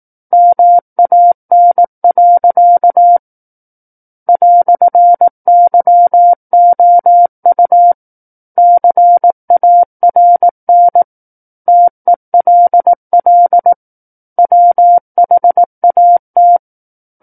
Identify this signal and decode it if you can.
Morse code, heard as a keyed tone, about 18 wpm, 700 Hz. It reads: MAN. "YOU CARN TELL WHAT
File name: 18 WPM